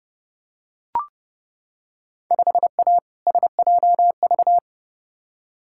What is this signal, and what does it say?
Morse code, heard as a keyed tone, about 32 wpm, 700 Hz. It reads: 5A SJV
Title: Field Day Exchanges - 30wpm